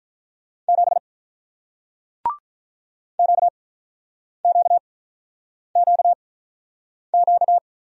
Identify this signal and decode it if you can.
Morse code, heard as a keyed tone, about 48 wpm, spaced as for 9 wpm, 700 Hz. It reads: 7 QQQQ